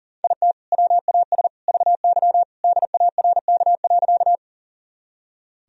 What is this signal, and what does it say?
Morse code, heard as a keyed tone, about 37 wpm, 700 Hz. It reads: IT WAS VY DARK.
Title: Sentences from Top 500 Words and Abbreviations 40wpm